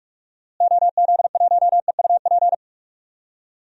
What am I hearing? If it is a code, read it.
Morse code, heard as a keyed tone, about 46 wpm, 700 Hz. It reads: OZ1EUP
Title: DXCC-100 Call Signs - 6 Characters - Repeated 2x - 45wpm